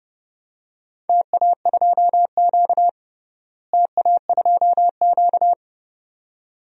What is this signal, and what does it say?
Morse code, heard as a keyed tone, about 31 wpm, 700 Hz. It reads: TA2Q TA2Q
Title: DXCC-100 Call Signs - 4 Characters - Repeated 2x - 30wpm